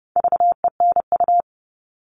Morse code, heard as a keyed tone, about 30 wpm, 700 Hz. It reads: VENU
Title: All Cut Numbers - 4-Digit 30wpm